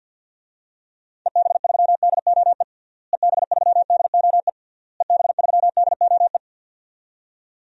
Morse code, heard as a keyed tone, about 51 wpm, 700 Hz. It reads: EB3DOE EB3DOE EB3DOE
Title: DXCC-100 Call Signs - 6 Characters - Repeated 3x - 50wpm